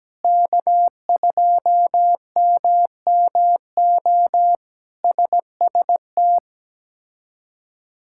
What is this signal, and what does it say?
Morse code, heard as a keyed tone, about 17 wpm, 700 Hz. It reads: K2MMO SST
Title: K1USN SST QSO Practice - 17wpm